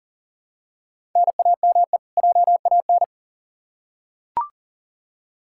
Morse code, heard as a keyed tone, about 40 wpm, 700 Hz. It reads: NAME JAN
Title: Practice QSO - Exchange 1 - Beginning - 40wpm